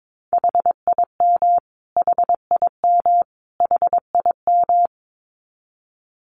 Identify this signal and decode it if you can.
Morse code, heard as a keyed tone, about 22 wpm, 700 Hz. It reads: HIM HIM HIM
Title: Top 200 Words Repeated 22wpm